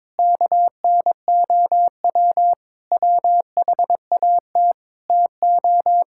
Morse code, heard as a keyed tone, about 22 wpm, 700 Hz. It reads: KNOW WHAT TO
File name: practice / 22wpm / Sentences with 3-Word Phrases - 22wpm